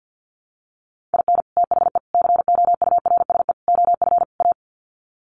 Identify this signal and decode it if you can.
Morse code, heard as a keyed tone, about 50 wpm, 700 Hz. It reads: IN THE COURSE OF A
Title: Sets of 5 Words Encore2 50wpm